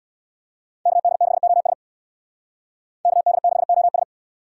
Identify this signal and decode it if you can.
Morse code, heard as a keyed tone, about 63 wpm, 700 Hz. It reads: KD6OS KD6OS